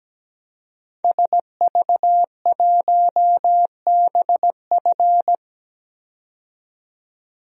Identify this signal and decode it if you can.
Morse code, heard as a keyed tone, about 17 wpm, 700 Hz. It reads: SV1BF
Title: DXCC-100 Call Signs - 5 Characters - 17wpm